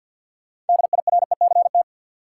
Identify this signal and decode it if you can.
Morse code, heard as a keyed tone, about 49 wpm, 700 Hz. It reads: DIRECT
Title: Sentences from Top 400 Words - New Word Drill - 50wpm